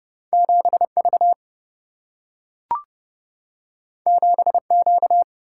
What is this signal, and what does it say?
Morse code, heard as a keyed tone, about 29 wpm, 700 Hz. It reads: 7 V 7 Q
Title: Single-Letter Single-Number 30wpm